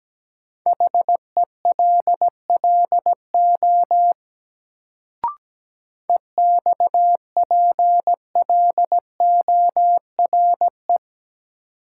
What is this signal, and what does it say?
Morse code, heard as a keyed tone, about 17 wpm, 700 Hz. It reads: HELLO EXPLORE
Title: Sentences from Top 1100 Words - New Word Drill Alt - 17wpm